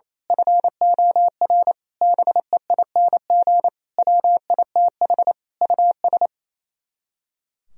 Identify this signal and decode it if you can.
Morse code, heard as a keyed tone, about 26 wpm, 700 Hz. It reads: FOR BEING WITH US
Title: Speed-Racing - Sets of 5 Words 28wpm